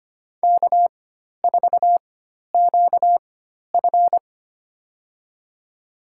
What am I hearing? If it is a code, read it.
Morse code, heard as a keyed tone, about 25 wpm, spaced as for 12 wpm, 700 Hz. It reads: K4QF
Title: USA Call Signs - 4 Characters - 4x Character Spacing - 25wpm